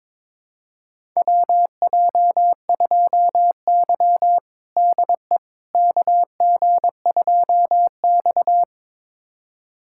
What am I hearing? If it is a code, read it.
Morse code, heard as a keyed tone, about 23 wpm, 700 Hz. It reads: WJ2Y DE KG2X
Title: USA Callsign DE Callsign - 4 Characters - 22wpm